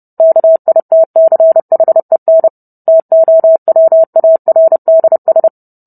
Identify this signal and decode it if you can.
Morse code, heard as a keyed tone, about 30 wpm, 650 Hz. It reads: KITCHEN TOWARDS